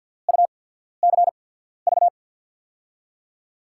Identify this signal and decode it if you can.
Morse code, heard as a keyed tone, about 53 wpm, spaced as for 14 wpm, 690 Hz. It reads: UCV
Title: USA Call Signs - 6 Characters - 8x Character Spacing - 50wpm